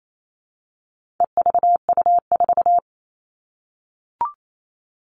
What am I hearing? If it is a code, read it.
Morse code, heard as a keyed tone, about 28 wpm, 700 Hz. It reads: EVU4
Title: All Cut Numbers - 4-Digit 28wpm